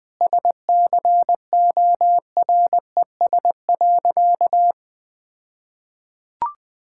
Morse code, heard as a keyed tone, about 20 wpm, 700 Hz. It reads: SCORES.
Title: All-Age Appropriate Jokes - 20wpm